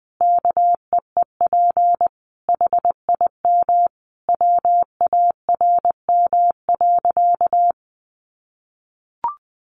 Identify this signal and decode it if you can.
Morse code, heard as a keyed tone, about 20 wpm, 700 Hz. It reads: KEEP HIM WARM.
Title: Sentences from Top 1500 Words 20wpm